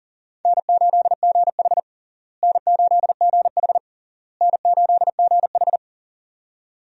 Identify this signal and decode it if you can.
Morse code, heard as a keyed tone, about 40 wpm, 700 Hz. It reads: N8GH N8GH N8GH
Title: USA Call Signs - 4 Characters - Repeated 3x - 40wpm